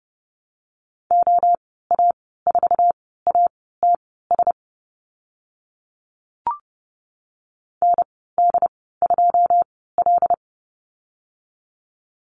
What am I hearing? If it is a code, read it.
Morse code, heard as a keyed tone, about 30 wpm, spaced as for 17 wpm, 700 Hz. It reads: OA4ATS ND2L
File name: Lesson 14g - Number 2 - Call Signs 30wpm